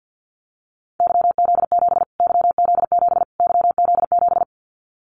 Morse code, heard as a keyed tone, about 50 wpm, 700 Hz. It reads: YZ7 YZ7 YZ7
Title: Callsign Prefixes for Top 25 DXCC - Repeated 3x - 50wpm